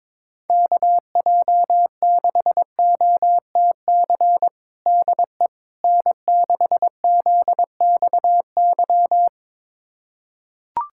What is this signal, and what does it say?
Morse code, heard as a keyed tone, about 22 wpm, 700 Hz. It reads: KJ6OTC DE N6ZXY
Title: USA Callsign DE Callsign 22wpm